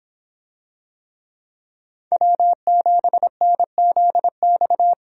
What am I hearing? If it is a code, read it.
Morse code, heard as a keyed tone, about 26 wpm, 700 Hz. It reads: W7NZX